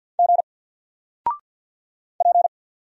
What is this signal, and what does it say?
Morse code, heard as a keyed tone, about 50 wpm, 700 Hz. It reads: G P
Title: Single Letter-Number 50wpm